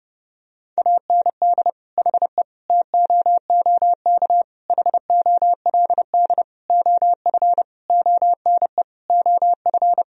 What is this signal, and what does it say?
Morse code, heard as a keyed tone, about 30 wpm, 700 Hz. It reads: AND HE TOOK HOLD OF ONE OF
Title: Sentences from Top 1000 Words - Review Part 4 - 30wpm